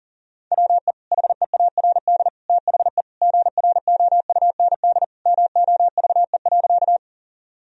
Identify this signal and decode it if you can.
Morse code, heard as a keyed tone, about 40 wpm, 700 Hz. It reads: WE HEARD THE GROUND MOVE.
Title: Sentences from Top 1000 Words - Review Part 2 - 40wpm